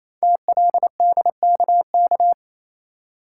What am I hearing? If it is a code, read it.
Morse code, heard as a keyed tone, about 28 wpm, 700 Hz. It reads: TLDKK
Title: Code Groups - 5 Letters - 28wpm